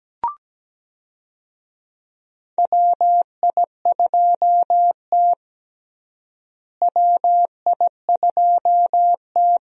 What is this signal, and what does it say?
Morse code, heard as a keyed tone, about 17 wpm, 700 Hz. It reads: WI2T WI2T
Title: USA Call Signs - 4 Characters - Repeated 2x - 17wpm